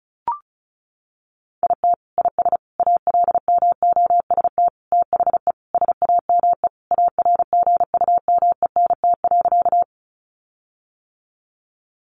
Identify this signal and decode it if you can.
Morse code, heard as a keyed tone, about 34 wpm, 700 Hz. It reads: IT IS ALMOST THE SAME ARGUMENT.
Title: Sentences from Top 1400 Words 35wpm